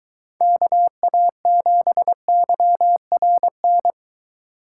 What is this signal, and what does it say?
Morse code, heard as a keyed tone, about 22 wpm, 700 Hz. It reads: KA7YRN